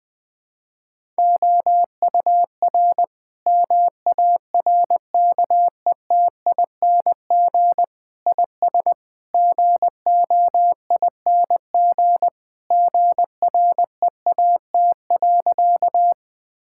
Morse code, heard as a keyed tone, about 20 wpm, 700 Hz. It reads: OUR MARKETING IS GOING GREAT.
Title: Sentences from Top 1000 Words - Review Part 5 - 20wpm